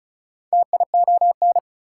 Morse code, heard as a keyed tone, about 48 wpm, spaced as for 42 wpm, 700 Hz. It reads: TION
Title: Speed-Racing Prefixes and Suffixes 35wpm